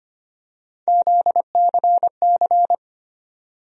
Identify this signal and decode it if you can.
Morse code, heard as a keyed tone, about 25 wpm, 700 Hz. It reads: ZCC
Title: USA Call Sign Postfixes - 3 Letters 25wpm